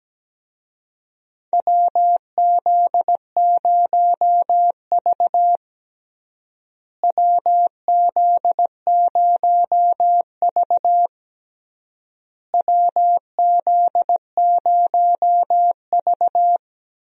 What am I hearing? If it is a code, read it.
Morse code, heard as a keyed tone, about 17 wpm, 700 Hz. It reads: WZ0V WZ0V WZ0V
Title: USA Call Signs - 4 Characters - Repeated 3x - 17wpm